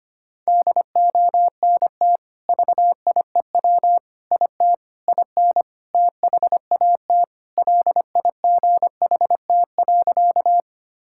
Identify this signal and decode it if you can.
Morse code, heard as a keyed tone, about 25 wpm, 700 Hz. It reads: DONT VIEW IT IN THAT LIGHT.
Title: Sentences from Top 1000 Words - Review Part 6 - 25wpm